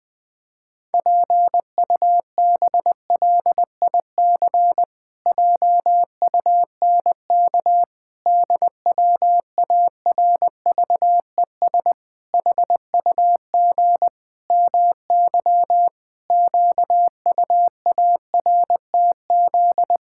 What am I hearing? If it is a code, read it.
Morse code, heard as a keyed tone, about 20 wpm, 700 Hz. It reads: PUBLIC JUNK DWARVES HUG MY QUARTZ
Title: Pangram Challenge v2 - 20wpm